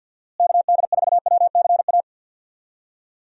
Morse code, heard as a keyed tone, about 50 wpm, 700 Hz. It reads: KD4WCA
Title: USA Call Signs - 6 Characters - Repeated 2x - 50wpm